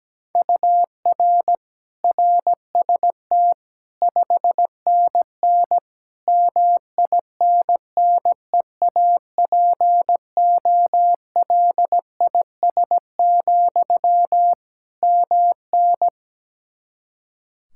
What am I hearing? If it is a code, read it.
Morse code, heard as a keyed tone, about 17 wpm, 700 Hz. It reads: UR RST 5NN MINNEAPOLIS, MN
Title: Practice QSO - Exchange 1 - Beginning - 17wpm